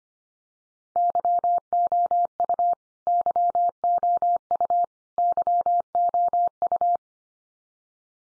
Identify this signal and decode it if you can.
Morse code, heard as a keyed tone, about 25 wpm, 700 Hz. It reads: YOU YOU YOU
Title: Top 100 Words 25wpm x3